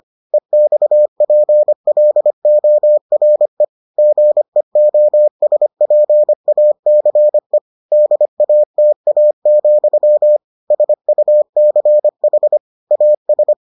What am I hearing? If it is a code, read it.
Morse code, heard as a keyed tone, about 25 wpm, 600 Hz. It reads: EXPLORE GEOSPACE DATA, SUCH AS